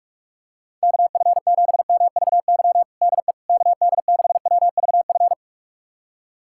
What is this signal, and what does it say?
Morse code, heard as a keyed tone, about 46 wpm, 690 Hz. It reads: KF7MVY DE KD6WVF
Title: USA Callsign DE Callsign - 6 Characters - 45wpm